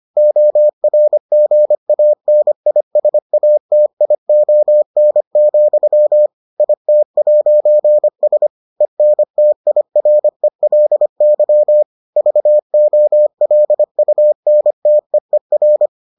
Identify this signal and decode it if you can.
Morse code, heard as a keyed tone, about 25 wpm, 600 Hz. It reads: ORGANISATION, IT'S ENTIRELY VOLUNTEER